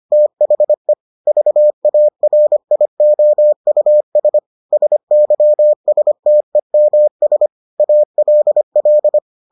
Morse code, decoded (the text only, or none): THE VARIOUS SYSTEMS ALL